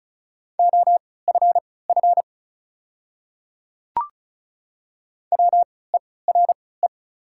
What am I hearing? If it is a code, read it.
Morse code, heard as a keyed tone, about 36 wpm, spaced as for 20 wpm, 700 Hz. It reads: OFF WERE
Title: 3x Character Spacing - Top 300 Words - 35wpm